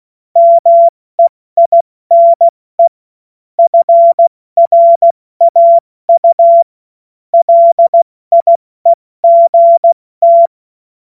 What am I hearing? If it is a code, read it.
Morse code, heard as a keyed tone, about 16 wpm, 680 Hz. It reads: MEINE FRAU LIEGT
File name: Morsekurs / Witze / Witz 03. August 2021, 70 F80